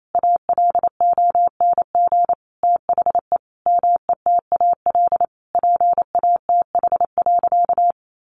Morse code, heard as a keyed tone, about 28 wpm, 700 Hz. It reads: ALONG THE METAL PATH.
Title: A Sound Of Thunder 28wpm